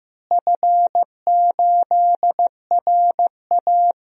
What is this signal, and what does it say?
Morse code, heard as a keyed tone, about 16 wpm, 700 Hz. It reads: F8RA
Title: DXCC-100 Call Signs - 4 Characters - Repeated 2x - 15wpm